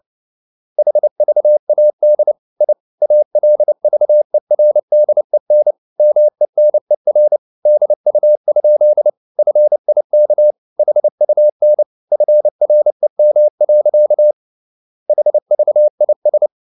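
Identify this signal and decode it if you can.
Morse code, heard as a keyed tone, about 29 wpm, 600 Hz. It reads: HVAD I ALVERDEN MENER DU? FIK HUN FREM. HVIS
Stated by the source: Fast (QRQ) Morse at 29 wpm, from CW med Gnister.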